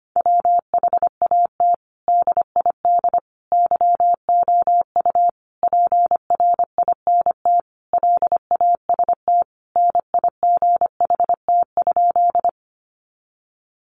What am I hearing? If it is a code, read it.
Morse code, heard as a keyed tone, about 25 wpm, 700 Hz. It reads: WHAT DID YOU PRINT LAST NIGHT?
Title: Sentences from Top 1400 Words 25wpm